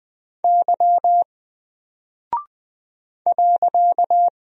Morse code, heard as a keyed tone, about 20 wpm, 700 Hz. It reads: Y .
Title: Single Character (Letter:Number:Punctuation:Prosign) Alt 20wpm